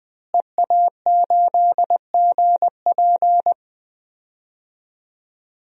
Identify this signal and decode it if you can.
Morse code, heard as a keyed tone, about 20 wpm, 700 Hz. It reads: EA8GP
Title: DXCC-100 Call Signs - 5 Characters - Repeated 2x - 20wpm